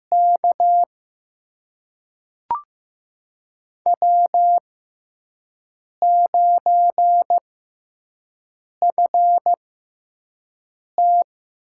Morse code, heard as a keyed tone, about 15 wpm, spaced as for 5 wpm, 700 Hz. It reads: K W9FT
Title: USA Call Signs - 4 Characters - 6x Character Spacing - 15wpm